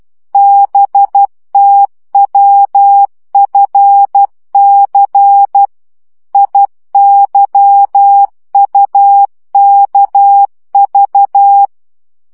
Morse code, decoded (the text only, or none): BTWFC IYUKV